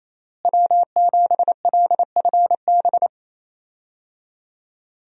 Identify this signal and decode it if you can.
Morse code, heard as a keyed tone, about 28 wpm, 700 Hz. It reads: W7LFB
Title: USA Call Signs - 5 Characters - Repeated 3x - 28wpm